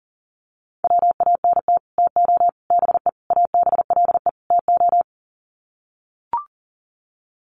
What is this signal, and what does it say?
Morse code, heard as a keyed tone, about 40 wpm, 700 Hz. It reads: WANT TO BE ABLE TO
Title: Sets of 5 Words 40wpm